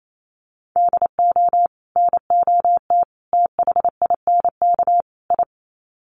Morse code, heard as a keyed tone, about 28 wpm, 700 Hz. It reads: DO NOT THINK I
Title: Sets of 4 Words 28wpm